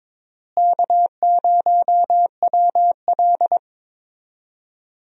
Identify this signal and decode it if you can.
Morse code, heard as a keyed tone, about 22 wpm, 700 Hz. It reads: K0WL
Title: USA Call Signs - 4 Characters - Repeated 3x - 22wpm